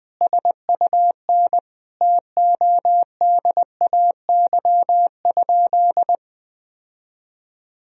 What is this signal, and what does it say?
Morse code, heard as a keyed tone, about 20 wpm, 700 Hz. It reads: SUN TODAY?
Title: Sentences from Top 1000 Words - Review Part 1 - 20wpm